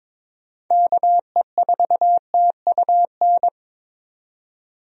Morse code, heard as a keyed tone, about 22 wpm, 700 Hz. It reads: KE4TUN
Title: USA Call Signs - 6 Characters - 22wpm